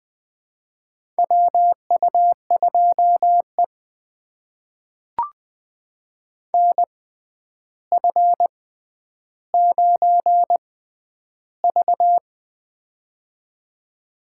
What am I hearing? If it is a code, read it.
Morse code, heard as a keyed tone, about 21 wpm, spaced as for 10 wpm, 700 Hz. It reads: WU2E NF9V
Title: USA Call Signs - 4 Characters - 6x Character Spacing - 20wpm